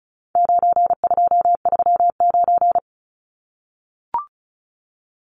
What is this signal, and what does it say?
Morse code, heard as a keyed tone, about 34 wpm, 700 Hz. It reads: 9239
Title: Numbers - 4-Digit 35wpm